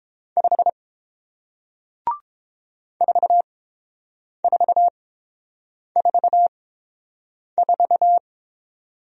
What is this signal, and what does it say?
Morse code, heard as a keyed tone, about 30 wpm, spaced as for 7 wpm, 700 Hz. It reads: 5 4444